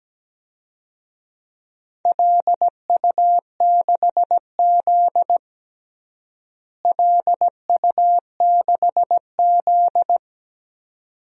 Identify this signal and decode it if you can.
Morse code, heard as a keyed tone, about 17 wpm, 700 Hz. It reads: LU6Z LU6Z
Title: DXCC-100 Call Signs - 4 Characters - Repeated 2x - 17wpm